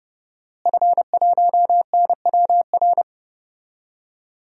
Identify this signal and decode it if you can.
Morse code, heard as a keyed tone, about 30 wpm, 700 Hz. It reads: F1NWR
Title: DXCC-100 Call Signs - 5 Characters - Repeated 2x - 30wpm